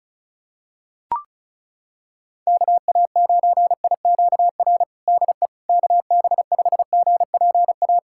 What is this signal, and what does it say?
Morse code, heard as a keyed tone, about 34 wpm, 700 Hz. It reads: KA9IQR DE KB5GPA
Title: USA Callsign DE Callsign - 6 Characters - 35wpm